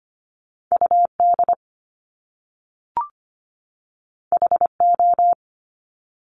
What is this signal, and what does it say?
Morse code, heard as a keyed tone, about 25 wpm, 700 Hz. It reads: U D H O
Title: Top Two Letter Combinations in Words 25wpm